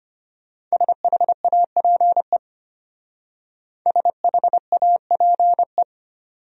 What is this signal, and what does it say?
Morse code, heard as a keyed tone, about 31 wpm, 700 Hz. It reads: SHAPE SHAPE
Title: Speed-Racing Top 500 Words 25wpm